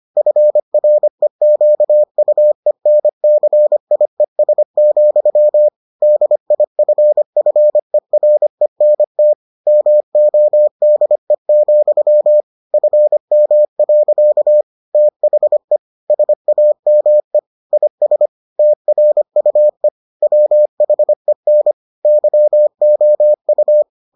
Morse code, heard as a keyed tone, about 25 wpm, 600 Hz. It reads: FREQUENCIES, DIFFERENT MODE, FM. THE SAME IS TRUE WHEN YOU